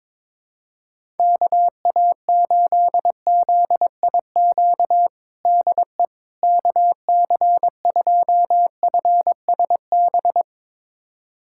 Morse code, heard as a keyed tone, about 22 wpm, 700 Hz. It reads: KA8ZIQ DE KC2FSB
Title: USA Callsign DE Callsign 22wpm